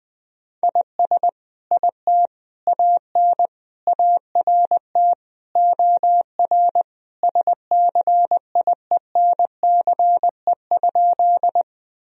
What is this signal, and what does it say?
Morse code, heard as a keyed tone, about 21 wpm, 700 Hz. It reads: IS IT AN ART OR SCIENCE?
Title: Sentences from Top 1000 Words - Review Part 3 - 20wpm